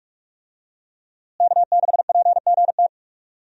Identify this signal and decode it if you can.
Morse code, heard as a keyed tone, about 46 wpm, 700 Hz. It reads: K6PGT
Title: USA Call Signs - 5 Characters - Repeated 3x - 45wpm